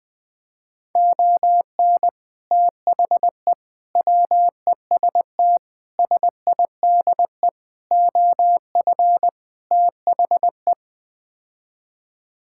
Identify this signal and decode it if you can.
Morse code, heard as a keyed tone, about 21 wpm, 700 Hz. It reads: ON THE WEST SIDE OF THE
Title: Sets of 6 Words - 20wpm